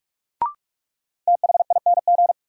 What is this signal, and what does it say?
Morse code, heard as a keyed tone, about 46 wpm, 700 Hz. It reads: THING
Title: Sets of 3 Words - New Word Drill - 45wpm